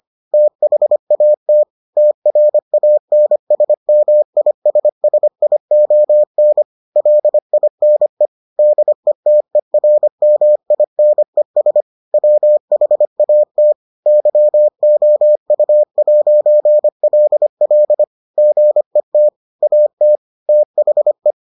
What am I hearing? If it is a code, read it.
Morse code, heard as a keyed tone, about 25 wpm, 600 Hz. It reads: THAT TRANSMISSION LINE DETERMINES WHAT YOU'LL GET AT THE